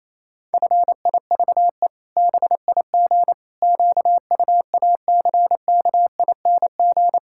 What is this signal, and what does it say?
Morse code, heard as a keyed tone, about 28 wpm, 700 Hz. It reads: FIVE BIG QUACKING
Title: Pangram Challenge v2 - 28wpm